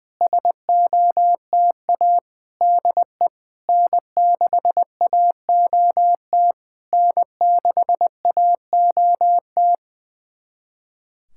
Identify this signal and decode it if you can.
Morse code, heard as a keyed tone, about 20 wpm, 690 Hz. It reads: SOTA DE N6AOT N6AOT
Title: SOTA QSO Practice 20wpm